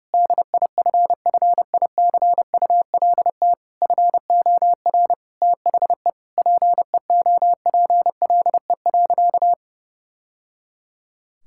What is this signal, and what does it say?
Morse code, heard as a keyed tone, about 30 wpm, 700 Hz. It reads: DIFFICULT FOR THE PEOPLE.
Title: Sentences from Top 1000 Words - Review Part 5 - 30wpm